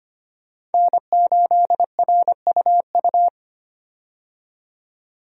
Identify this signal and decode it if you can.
Morse code, heard as a keyed tone, about 25 wpm, 700 Hz. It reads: N8RUU
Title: USA Call Signs - 5 Characters - 25wpm